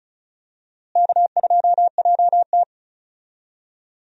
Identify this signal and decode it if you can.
Morse code, heard as a keyed tone, about 36 wpm, 700 Hz. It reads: K2JT
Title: USA Call Signs - 4 Characters - Repeated 3x - 35wpm